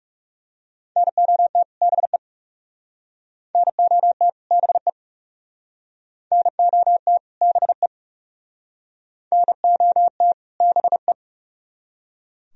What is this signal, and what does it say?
Morse code, heard as a keyed tone, about 38 wpm, 700 Hz. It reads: NOT BE NOT BE NOT BE NOT BE